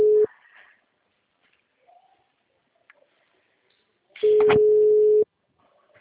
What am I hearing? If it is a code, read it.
Ringback tone (Europe, 425 Hz).
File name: out- -2016- - - .129